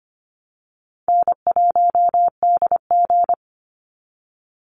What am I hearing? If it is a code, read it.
Morse code, heard as a keyed tone, about 25 wpm, 700 Hz. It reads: N1DG
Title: Call Signs Repeated 3x - 25wpm